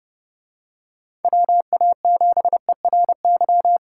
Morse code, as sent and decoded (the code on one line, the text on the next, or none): .-- .- --... .-. -.--
WA7ERY